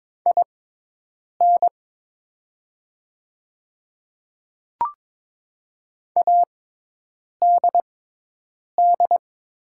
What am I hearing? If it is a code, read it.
Morse code, heard as a keyed tone, about 22 wpm, spaced as for 8 wpm, 700 Hz. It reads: IN ADD
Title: 6x Character Spacing - Top 500 Words - 22wpm